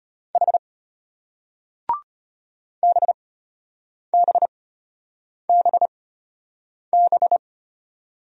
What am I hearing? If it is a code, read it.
Morse code, heard as a keyed tone, about 33 wpm, spaced as for 8 wpm, 700 Hz. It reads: H BBBB